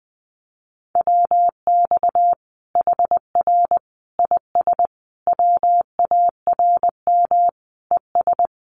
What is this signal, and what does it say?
Morse code, heard as a keyed tone, about 20 wpm, 700 Hz. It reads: WX HR IS WARM ES
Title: Common QSO Phrases 20wpm